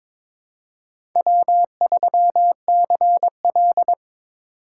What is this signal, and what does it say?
Morse code, heard as a keyed tone, about 22 wpm, 700 Hz. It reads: W3CL
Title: USA Call Signs - 4 Characters - 4x Character Spacing - 22wpm